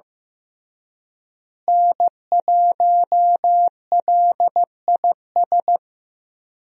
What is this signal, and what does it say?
Morse code, heard as a keyed tone, about 15 wpm, 700 Hz. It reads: N1LIS
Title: USA Call Signs - 5 Characters - Repeated 3x - 15wpm